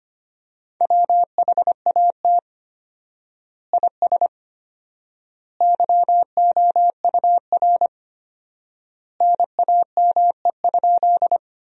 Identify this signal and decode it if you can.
Morse code, heard as a keyed tone, about 25 wpm, 700 Hz. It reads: WHAT IS YOUR NAME?
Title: 4x Word-Spacing - Sentences from Top 200 Words - 25wpm